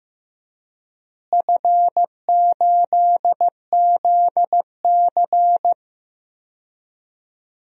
Morse code, heard as a keyed tone, about 15 wpm, 700 Hz. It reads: F8ZC
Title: DXCC-100 Call Signs - 4 Characters - Repeated 3x - 15wpm